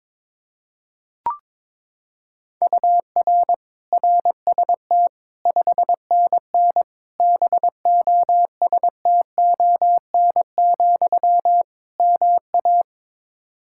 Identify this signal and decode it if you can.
Morse code, heard as a keyed tone, about 22 wpm, 700 Hz. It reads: UR RST 5NN BOSTON, MA
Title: Practice QSO - Exchange 1 - Beginning - 22wpm